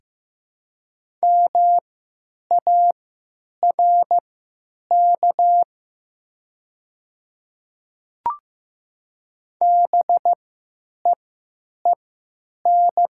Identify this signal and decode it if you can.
Morse code, heard as a keyed tone, about 15 wpm, spaced as for 9 wpm, 700 Hz. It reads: MARK BEEN
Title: 3x Character Spacing - Top 300 Words - 15wpm